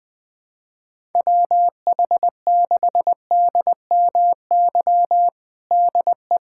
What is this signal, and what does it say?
Morse code, heard as a keyed tone, about 20 wpm, 700 Hz. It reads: WH6DMY DE
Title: USA Callsign DE Callsign - 6 Characters - 20wpm